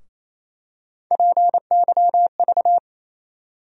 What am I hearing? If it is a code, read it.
Morse code, heard as a keyed tone, about 28 wpm, 700 Hz. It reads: PYV
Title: USA Call Sign Postfixes - 3 Letters 28wpm